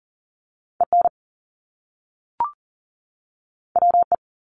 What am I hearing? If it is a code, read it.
Morse code, heard as a keyed tone, about 38 wpm, 700 Hz. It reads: EN WE
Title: Top Two Letter Combinations in Words 40wpm